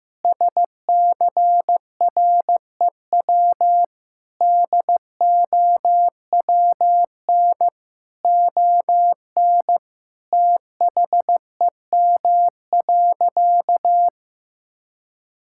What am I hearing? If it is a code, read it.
Morse code, heard as a keyed tone, about 15 wpm, 700 Hz. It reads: SCREW DOWN ON THEM.
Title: Sentences from Top 1600 Words 15wpm